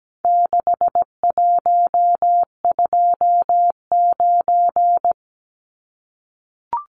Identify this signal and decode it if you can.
Morse code, heard as a keyed tone, about 17 wpm, 700 Hz. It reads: 6129
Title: Numbers - 4-Digit 17wpm